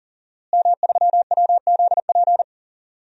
Morse code, heard as a keyed tone, about 40 wpm, 700 Hz. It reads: M3WZP
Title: DXCC-100 Call Signs - 5 Characters - Repeated 2x - 40wpm